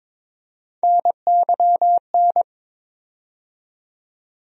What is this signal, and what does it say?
Morse code, heard as a keyed tone, about 22 wpm, 700 Hz. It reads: NYN
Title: USA Call Sign Postfixes - 3 Letters 22wpm